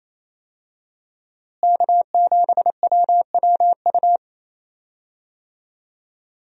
Morse code, heard as a keyed tone, about 28 wpm, 700 Hz. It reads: K7WWU
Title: USA Call Signs - 5 Characters - Repeated 2x - 28wpm